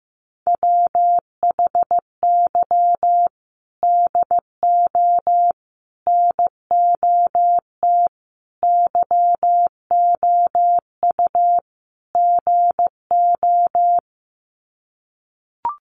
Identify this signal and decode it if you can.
Morse code, heard as a keyed tone, about 15 wpm, 700 Hz. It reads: WHY DO NOT YOU GO
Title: Sets of 5 Words Encore 15wpm